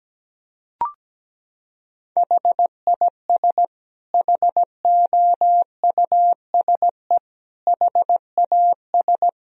Morse code, heard as a keyed tone, about 17 wpm, 700 Hz. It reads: HIS HOUSE HAS
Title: Sentences from Top 1000 Words - Review Part 3 - 17wpm